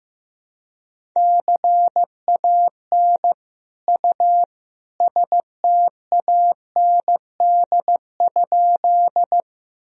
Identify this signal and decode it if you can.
Morse code, heard as a keyed tone, about 15 wpm, 700 Hz. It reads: CAN U STAND?
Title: Sentences from Top 300 Words and Abbreviations 15wpm